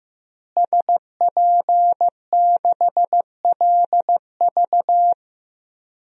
Morse code, heard as a keyed tone, about 16 wpm, 700 Hz. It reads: SP6LV
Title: DXCC-100 Call Signs - 5 Characters - Repeated 2x - 15wpm